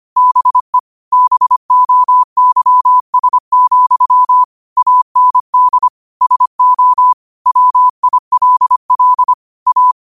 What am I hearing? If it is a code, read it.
Morse code, heard as a keyed tone, about 25 wpm, 1000 Hz. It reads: BE BOYS, AND SO WILL A
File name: qotdfiles / 25wpm